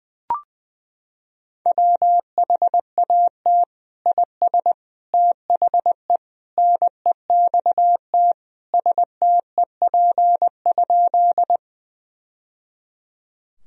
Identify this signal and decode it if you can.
Morse code, heard as a keyed tone, about 21 wpm, 700 Hz. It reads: WHAT IS THE NEXT STEP?
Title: Sentences from Top 1000 Words - Review Part 2 - 20wpm